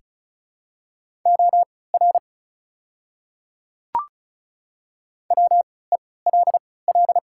Morse code, heard as a keyed tone, about 36 wpm, spaced as for 20 wpm, 700 Hz. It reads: OR WELL
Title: 3x Character Spacing - Top 200 Words - 35wpm